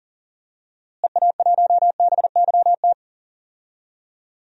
Morse code, heard as a keyed tone, about 40 wpm, 700 Hz. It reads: EA1BYT
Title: DXCC-100 Call Signs - 6 Characters - Repeated 3x - 40wpm